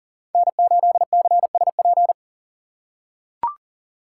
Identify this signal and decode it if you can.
Morse code, heard as a keyed tone, about 40 wpm, 700 Hz. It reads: N8CSP
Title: USA Call Signs - 3x Character Spacing - 40wpm